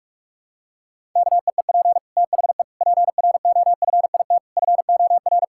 Morse code, heard as a keyed tone, about 45 wpm, 700 Hz. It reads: KEEP THE PROFIT FOR
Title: Sentences from Top 1000 Words - Review Part 5 - 45wpm